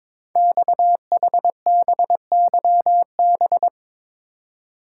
Morse code, heard as a keyed tone, about 22 wpm, 700 Hz. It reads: XHBYB
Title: Code Groups - 5 Letters - 22wpm